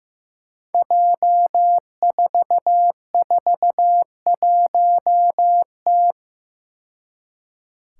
Morse code, heard as a keyed tone, about 15 wpm, 700 Hz. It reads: J441T
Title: Five Challenging Letter-Number Combinations 15wpm